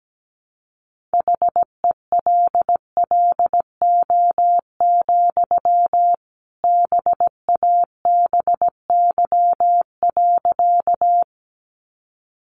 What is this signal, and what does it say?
Morse code, heard as a keyed tone, about 17 wpm, 700 Hz. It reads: HELLO, BABY.
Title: Sentences from Top 1300 Words 17wpm